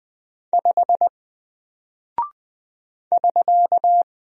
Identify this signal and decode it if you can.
Morse code, heard as a keyed tone, about 20 wpm, 700 Hz. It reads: 5 <SK>
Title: Single Character (Letter:Number:Punctuation:Prosign) Alt 20wpm